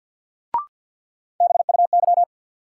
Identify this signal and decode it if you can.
Morse code, heard as a keyed tone, about 53 wpm, 700 Hz. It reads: BUY
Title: Sentences from Top 700 Words - New Word Drill - 50wpm